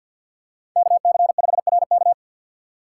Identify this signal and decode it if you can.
Morse code, heard as a keyed tone, about 52 wpm, 700 Hz. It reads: KC5RK
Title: USA Call Signs - 5 Characters - 4x Character Spacing - 50wpm